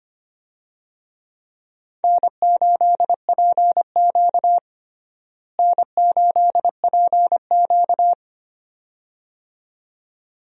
Morse code, heard as a keyed tone, about 25 wpm, 700 Hz. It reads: N8PQ N8PQ
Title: USA Call Signs - 4 Characters - Repeated 2x - 25wpm